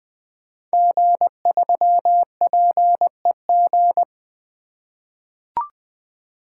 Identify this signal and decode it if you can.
Morse code, heard as a keyed tone, about 20 wpm, 700 Hz. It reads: G3PEG
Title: Code Groups - 5 Letter:Numbers - 20wpm